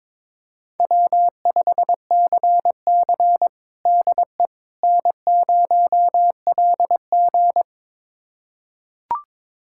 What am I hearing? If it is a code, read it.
Morse code, heard as a keyed tone, about 22 wpm, 700 Hz. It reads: W5CC DE N0LG
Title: USA Callsign DE Callsign - 4 Characters - 22wpm